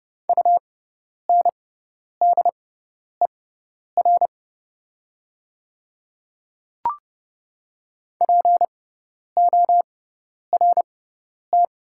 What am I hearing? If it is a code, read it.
Morse code, heard as a keyed tone, about 30 wpm, spaced as for 10 wpm, 700 Hz. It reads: UNDER PORT
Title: 6x Character Spacing - Top 200 Words - 30wpm